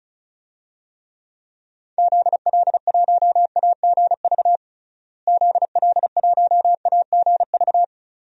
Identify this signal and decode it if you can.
Morse code, heard as a keyed tone, about 34 wpm, 700 Hz. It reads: ZL1AGV ZL1AGV
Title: DXCC-100 Call Signs - 6 Characters - Repeated 3x - 35wpm